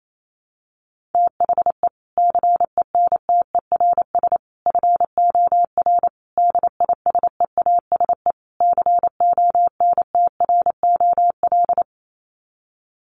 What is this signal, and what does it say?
Morse code, heard as a keyed tone, about 28 wpm, 700 Hz. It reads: THE CENTERS FOR DISEASE CONTROL
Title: Sets of 5 Words Encore2 28wpm